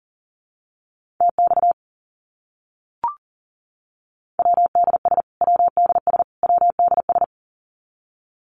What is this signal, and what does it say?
Morse code, heard as a keyed tone, about 40 wpm, 700 Hz. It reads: TX WDS WDS WDS
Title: Abbreviations Repeated 3x - 40wpm